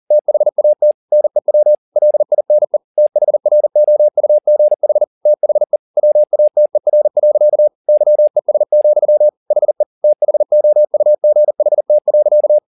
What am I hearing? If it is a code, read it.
Morse code, heard as a keyed tone, about 40 wpm, 600 Hz. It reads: THAT NEW LINE THROUGH THE WATER. YES, HE THOUGHT.
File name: the old man and the sea 2 40wpm